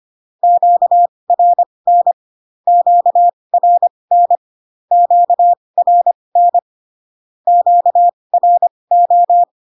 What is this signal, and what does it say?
Morse code, heard as a keyed tone, about 25 wpm, spaced as for 20 wpm, 700 Hz. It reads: QRN QRN QRN QRO
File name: cw / Q-codes most common repeated 3 times 15 25wpm ik1hge